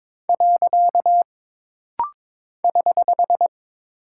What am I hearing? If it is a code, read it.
Morse code, heard as a keyed tone, about 22 wpm, 700 Hz. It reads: . <HH>
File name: practice / 22wpm / Single Punctuation-Prosign - Rapid-Fire - 22wpm